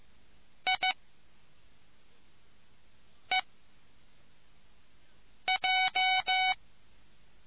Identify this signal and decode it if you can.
Morse code, heard as a keyed tone, about 15 wpm, spaced as for 4 wpm, 2250 Hz. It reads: I EJ